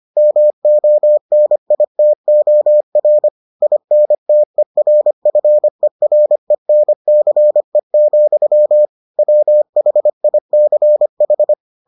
Morse code, heard as a keyed tone, about 25 wpm, 600 Hz. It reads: MONITOR INTERFERENCE, WHICH